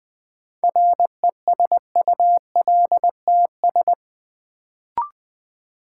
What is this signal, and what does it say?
Morse code, heard as a keyed tone, about 20 wpm, 700 Hz. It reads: RESULTS
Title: Sets of 4 Words - New Word Drill - 20wpm